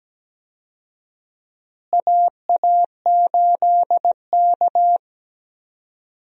Morse code, heard as a keyed tone, about 17 wpm, 700 Hz. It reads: AA8K
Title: USA Call Signs - 4 Characters - Repeated 3x - 17wpm